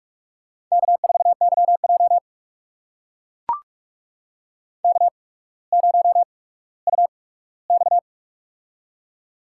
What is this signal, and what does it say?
Morse code, heard as a keyed tone, about 44 wpm, spaced as for 19 wpm, 700 Hz. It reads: K4YJ K0UX
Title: USA Call Signs - 4 Characters - 8x Character Spacing - 45wpm